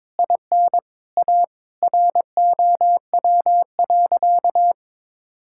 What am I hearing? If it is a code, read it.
Morse code, heard as a keyed tone, about 22 wpm, 700 Hz. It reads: IN A ROW.
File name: Sentences with 7-Word Phrases - 22wpm